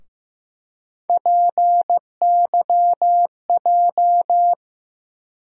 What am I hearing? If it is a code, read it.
Morse code, heard as a keyed tone, about 15 wpm, 700 Hz. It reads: PYJ
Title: USA Call Sign Postfixes - 3 Letters 15wpm